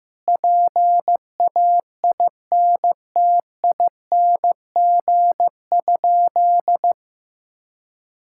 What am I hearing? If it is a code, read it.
Morse code, heard as a keyed tone, about 15 wpm, 700 Hz. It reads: PAINTING?
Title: Sentences from Top 1000 Words - Review Part 6 - 15wpm